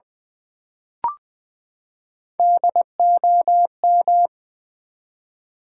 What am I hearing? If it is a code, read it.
Morse code, heard as a keyed tone, about 20 wpm, 700 Hz. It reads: DOM
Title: Build A Word - 3 Letters At A Time - 9-Letter Words - 20wpm